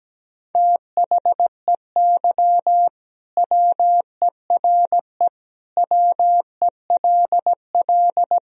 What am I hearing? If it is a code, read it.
Morse code, heard as a keyed tone, about 17 wpm, 700 Hz. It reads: THEY WERE WELL
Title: Sentences from Top 1000 Words - Review Part 3 - 17wpm